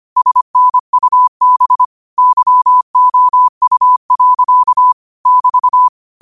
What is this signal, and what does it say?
Morse code, heard as a keyed tone, about 25 wpm, 1000 Hz. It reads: SNUB YOU. <BT>
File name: qotdfiles / 25wpm